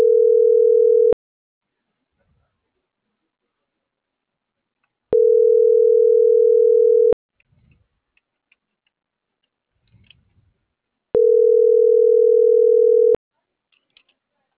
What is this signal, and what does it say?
Ringback tone (North America), 440 + 480 Hz.